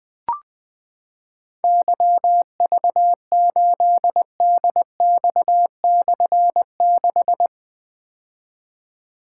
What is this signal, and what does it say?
Morse code, heard as a keyed tone, about 20 wpm, 700 Hz. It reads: YV8DX/6
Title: Difficult Call Signs 20wpm